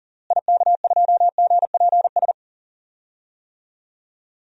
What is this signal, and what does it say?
Morse code, heard as a keyed tone, about 40 wpm, 700 Hz. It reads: IK2GPS
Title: DXCC-100 Call Signs - 6 Characters - Repeated 2x - 40wpm